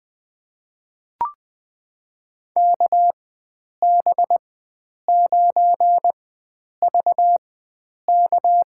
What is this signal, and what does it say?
Morse code, heard as a keyed tone, about 20 wpm, spaced as for 9 wpm, 700 Hz. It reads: KB9VK
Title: USA Call Signs - 5 Characters - 4x Character Spacing - 20wpm